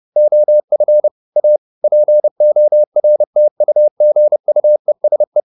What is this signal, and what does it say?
Morse code, heard as a keyed tone, about 30 wpm, 600 Hz. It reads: OF A PORTUGUESE
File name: the old man and the sea / the old man and the sea 1 30wpm